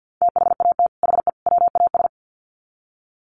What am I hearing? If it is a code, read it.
Morse code, heard as a keyed tone, about 50 wpm, 700 Hz. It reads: THAT HE WAS
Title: Sets of 3 Words 50wpm